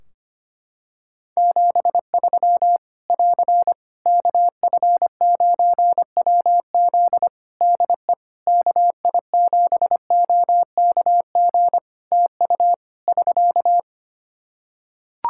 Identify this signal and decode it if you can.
Morse code, heard as a keyed tone, about 25 wpm, 700 Hz. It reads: 73 <AR> KF9WZ DE KI7OKG TU <SK>
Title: K1USN SST QSO Practice 25wpm